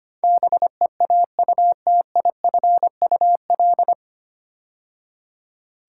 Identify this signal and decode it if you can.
Morse code, heard as a keyed tone, about 25 wpm, 700 Hz. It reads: BEAUTIFUL
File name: practice / 25wpm / 9 Letter Words - First 3 Letters Given - 25wpm